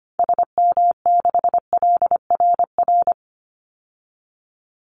Morse code, heard as a keyed tone, about 25 wpm, 700 Hz. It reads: SM6LRR
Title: Call Signs Repeated 2x - 25wpm